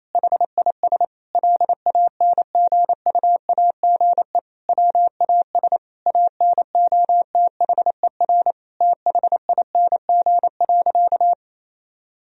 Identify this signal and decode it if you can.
Morse code, heard as a keyed tone, about 27 wpm, 700 Hz. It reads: HIS LANGUAGE WAS ANOTHER THING.
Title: Sentences from Top 1000 Words - Review Part 3 - 28wpm